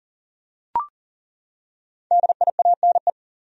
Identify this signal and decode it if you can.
Morse code, heard as a keyed tone, about 40 wpm, 700 Hz. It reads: DIANE
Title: USA Common Names - Max 5 Letters - 40wpm